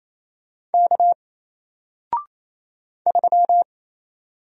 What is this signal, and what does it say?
Morse code, heard as a keyed tone, about 27 wpm, 700 Hz. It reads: K 3
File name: Single-Letter-Number-28wpm